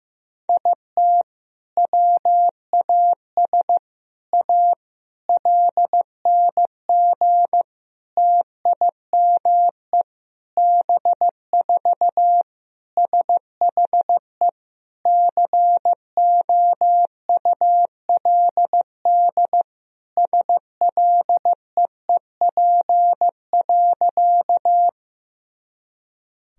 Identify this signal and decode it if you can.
Morse code, heard as a keyed tone, about 15 wpm, 700 Hz. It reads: IT WAS A LNG TIME B4 SHE COULD SLEEP.
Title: Sentences from Top 500 Words and Abbreviations 15wpm